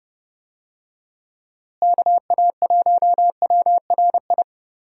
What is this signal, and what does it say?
Morse code, heard as a keyed tone, about 30 wpm, 700 Hz. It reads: KA1WRI
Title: USA Call Signs - 6 Characters - Repeated 2x - 30wpm